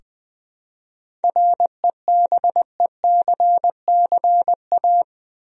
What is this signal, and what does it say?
Morse code, heard as a keyed tone, about 21 wpm, 700 Hz. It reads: REBECCA
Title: USA Common Names - 20wpm